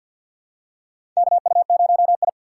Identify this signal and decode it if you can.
Morse code, heard as a keyed tone, about 53 wpm, 700 Hz. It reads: KU0I
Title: USA Call Signs - 4 Characters - 3x Character Spacing - 50wpm